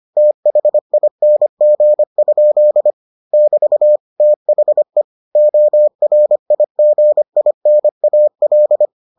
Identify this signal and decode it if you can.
Morse code, heard as a keyed tone, about 25 wpm, 600 Hz. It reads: THING? <BT> THE ORIGINAL